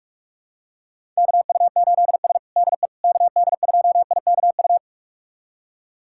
Morse code, heard as a keyed tone, about 46 wpm, 700 Hz. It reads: KU8S DE KD2IKU
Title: USA Callsign DE Callsign 45wpm